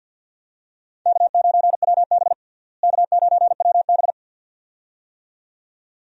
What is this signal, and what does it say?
Morse code, heard as a keyed tone, about 50 wpm, 700 Hz. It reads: K9WB K9WB
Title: USA Call Signs - 4 Characters - Repeated 2x - 50wpm